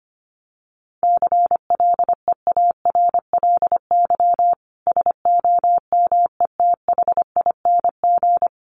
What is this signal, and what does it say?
Morse code, heard as a keyed tone, about 25 wpm, 700 Hz. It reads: CLEARLY SOMETHING
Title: Sentences from Top 1300 Words 25wpm